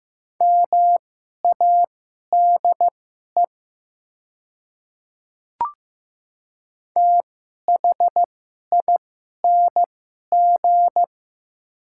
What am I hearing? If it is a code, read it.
Morse code, heard as a keyed tone, about 15 wpm, spaced as for 11 wpm, 700 Hz. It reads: MADE THING
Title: 2x Character Spacing - Top 300 Words - 15wpm